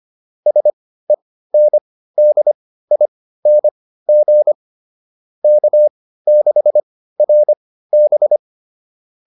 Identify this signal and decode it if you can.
Morse code, heard as a keyed tone, about 25 wpm, spaced as for 15 wpm, 600 Hz. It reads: SENDING K6RB